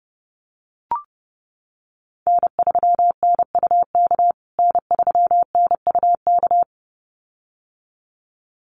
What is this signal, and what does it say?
Morse code, heard as a keyed tone, about 30 wpm, 700 Hz. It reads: N3NUK N3NUK
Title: USA Call Signs Repeated 2x 30wpm